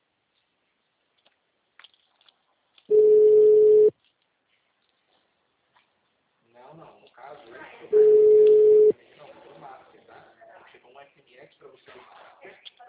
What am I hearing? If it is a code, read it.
Ringback tone (Europe, 425 Hz), with speech.
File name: out- -2011- - - .16134